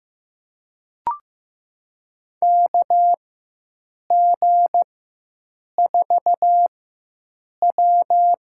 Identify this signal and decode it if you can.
Morse code, heard as a keyed tone, about 15 wpm, spaced as for 7 wpm, 700 Hz. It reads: KG4W
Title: USA Call Signs - 6 Characters - 4x Character Spacing - 15wpm